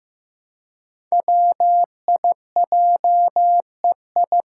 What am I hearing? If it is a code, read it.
Morse code, heard as a keyed tone, about 15 wpm, 700 Hz. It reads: WIJEI
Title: Five Challenging Letter-Number Combinations 15wpm